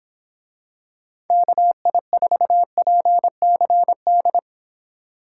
Morse code, heard as a keyed tone, about 26 wpm, 700 Hz. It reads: KI4PCD
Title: Speed Racing - USA 2x3 Call Signs 22wpm